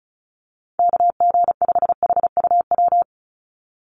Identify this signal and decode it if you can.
Morse code, heard as a keyed tone, about 34 wpm, 700 Hz. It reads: KG5HUW
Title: USA Call Signs 35wpm